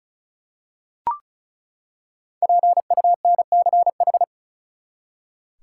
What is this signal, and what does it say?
Morse code, heard as a keyed tone, about 36 wpm, 700 Hz. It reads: PUNCH
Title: Five Letter Words - 35wpm